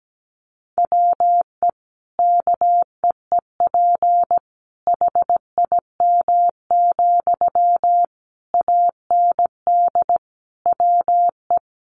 Morse code, heard as a keyed tone, about 17 wpm, 700 Hz. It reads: WE KEEP HIM, AND WE
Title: Sentences from Top 600 Words 17wpm